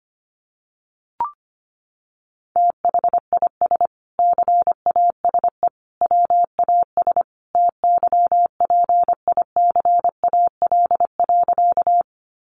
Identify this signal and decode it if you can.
Morse code, heard as a keyed tone, about 25 wpm, 700 Hz. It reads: THIS CASE WAS TYPICAL.
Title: Sentences from Top 1100 Words 25wpm